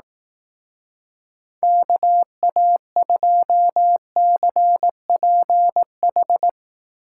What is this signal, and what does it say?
Morse code, heard as a keyed tone, about 19 wpm, 700 Hz. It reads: KA2CPH
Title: Speed Racing - USA 2x3 Call Signs 15wpm